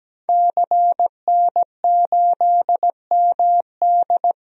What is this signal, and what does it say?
Morse code, heard as a keyed tone, about 17 wpm, 700 Hz. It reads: CN8MD
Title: DXCC-100 Call Signs - 5 Characters - Repeated 2x - 17wpm